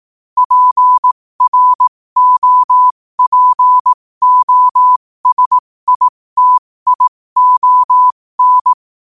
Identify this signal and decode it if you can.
Morse code, heard as a keyed tone, about 18 wpm, spaced as for 15 wpm, 1000 Hz. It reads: PROPOSITION
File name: qotdfiles / 15wpm